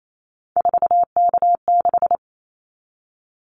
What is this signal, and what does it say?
Morse code, heard as a keyed tone, about 28 wpm, 700 Hz. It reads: 4K6
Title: Callsign Prefixes for Top 100 DXCC - 28wpm